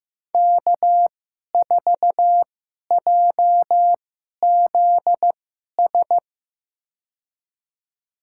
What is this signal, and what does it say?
Morse code, heard as a keyed tone, about 15 wpm, spaced as for 11 wpm, 700 Hz. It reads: K4JZS
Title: USA Call Signs - 5 Characters - 2x Character Spacing - 15wpm